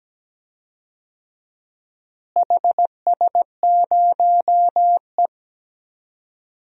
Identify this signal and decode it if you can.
Morse code, heard as a keyed tone, about 17 wpm, 700 Hz. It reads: HS0E
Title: DXCC-100 Call Signs - 4 Characters - Repeated 3x - 17wpm